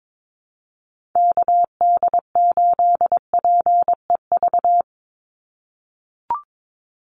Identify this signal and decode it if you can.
Morse code, heard as a keyed tone, about 22 wpm, 700 Hz. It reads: KD8PEV
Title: USA Call Signs 22wpm